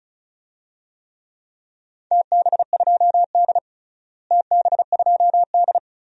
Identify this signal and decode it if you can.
Morse code, heard as a keyed tone, about 36 wpm, 700 Hz. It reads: TB2D TB2D
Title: DXCC-100 Call Signs - 4 Characters - Repeated 2x - 35wpm